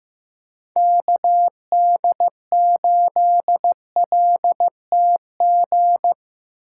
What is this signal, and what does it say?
Morse code, heard as a keyed tone, about 17 wpm, 700 Hz. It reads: KD8LTG
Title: Speed Racing - USA 2x3 Call Signs 15wpm